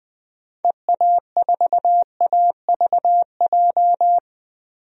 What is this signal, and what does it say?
Morse code, heard as a keyed tone, about 21 wpm, 700 Hz. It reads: EA4AVJ
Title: DXCC-100 Call Signs - 6 Characters - 20wpm